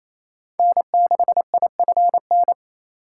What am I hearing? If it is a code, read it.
Morse code, heard as a keyed tone, about 28 wpm, 700 Hz. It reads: N6IFN
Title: USA Call Signs - 5 Characters - 2x Character Spacing - 28wpm